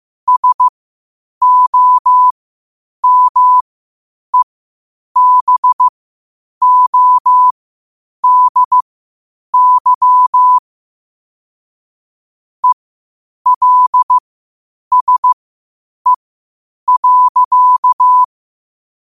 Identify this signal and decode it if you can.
Morse code, heard as a keyed tone, about 15 wpm, spaced as for 8 wpm, 1000 Hz. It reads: SOMEBODY ELSE.
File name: qotdfiles / 8wpm